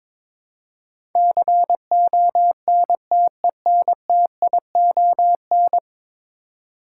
Morse code, heard as a keyed tone, about 21 wpm, 700 Hz. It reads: CONTENTION
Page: morse-code-ninja-practice-22wpm